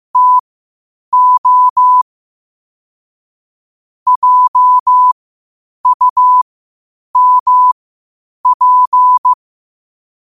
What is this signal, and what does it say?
Morse code, heard as a keyed tone, about 15 wpm, spaced as for 8 wpm, 1000 Hz. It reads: TO JUMP
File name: qotdfiles / 8wpm